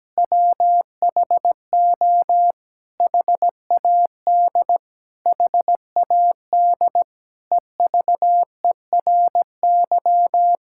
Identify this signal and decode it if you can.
Morse code, heard as a keyed tone, about 17 wpm, 700 Hz. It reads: WHO HAD HAD EVERY
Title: Sentences from Top 1000 Words - Review Part 6 - 17wpm